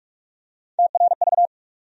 Morse code, heard as a keyed tone, about 41 wpm, 700 Hz. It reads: TRV
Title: USA Call Sign Postfixes - 3 Letters 45wpm